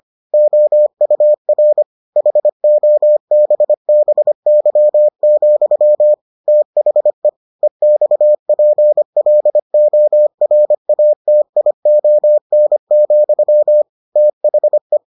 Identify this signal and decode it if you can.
Morse code, heard as a keyed tone, about 25 wpm, 600 Hz. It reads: OUR HOBBY, THE EXPLORATION, THE